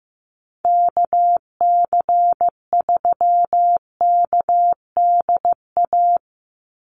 Morse code, heard as a keyed tone, about 15 wpm, 700 Hz. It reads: KC3KDA
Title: USA Call Signs 15wpm